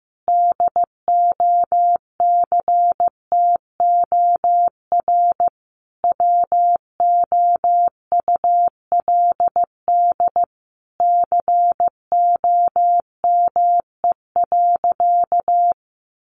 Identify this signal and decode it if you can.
Morse code, heard as a keyed tone, about 15 wpm, 700 Hz. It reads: DOCTOR WOULD COME.
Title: Sentences from Top 1500 Words 15wpm